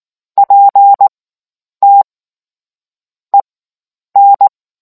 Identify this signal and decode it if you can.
Morse code, heard as a keyed tone, about 19 wpm, spaced as for 9 wpm, 800 Hz. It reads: PT EN